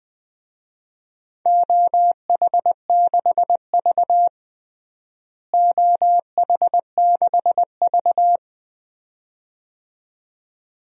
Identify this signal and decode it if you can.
Morse code, heard as a keyed tone, about 20 wpm, 700 Hz. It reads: OH6V OH6V
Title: DXCC-100 Call Signs - 4 Characters - Repeated 2x - 20wpm